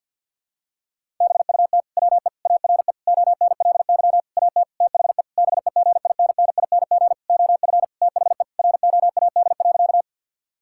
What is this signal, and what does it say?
Morse code, heard as a keyed tone, about 47 wpm, 700 Hz. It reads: BUT WE ARE ONLY AT THE BEGINNING OF THE ROAD.
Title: Sentences from Top 1000 Words - Review Part 5 - 50wpm